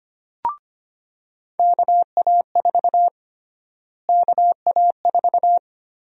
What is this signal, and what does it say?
Morse code, heard as a keyed tone, about 25 wpm, 700 Hz. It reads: KA4 KA4
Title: USA Call Sign Prefixes - Repeated 2x - 25wpm